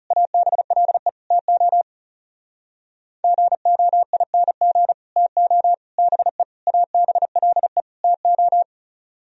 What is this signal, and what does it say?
Morse code, heard as a keyed tone, about 40 wpm, 700 Hz. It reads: ABLE TO GOING TO BE ABLE TO